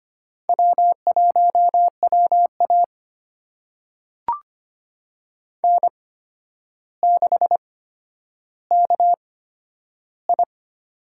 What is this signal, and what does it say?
Morse code, heard as a keyed tone, about 25 wpm, spaced as for 11 wpm, 700 Hz. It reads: W1WA N6KI
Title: USA Call Signs - 4 Characters - 8x Character Spacing - 25wpm